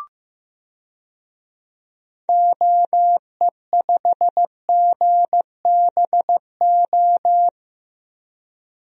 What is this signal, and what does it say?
Morse code, heard as a keyed tone, about 15 wpm, 700 Hz. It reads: OE5GBO
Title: DXCC-100 Call Signs - 6 Characters - Repeated 2x - 15wpm